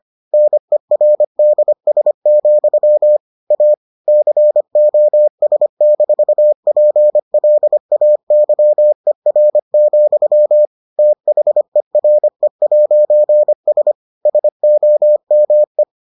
Morse code, heard as a keyed tone, about 25 wpm, 600 Hz. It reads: NERDS, A COS-PLAYER, THERE'S SOME